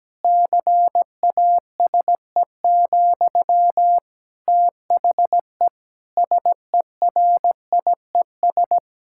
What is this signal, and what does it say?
Morse code, heard as a keyed tone, about 17 wpm, 700 Hz. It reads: CASE, THE SERIES
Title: Sentences from Top 1000 Words - Review Part 5 - 17wpm